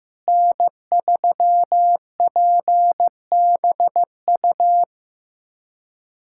Morse code, heard as a keyed tone, about 15 wpm, 700 Hz. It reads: N3PBU
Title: USA Callsign DE Callsign - 5 Characters - 15wpm